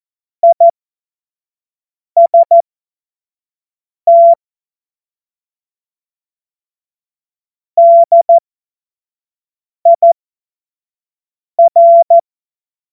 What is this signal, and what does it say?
Morse code, heard as a keyed tone, about 14 wpm, spaced as for 5 wpm, 680 Hz. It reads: IST DIR
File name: Morsekurs / Witze / Witz 04. Oktober 2022 25 F70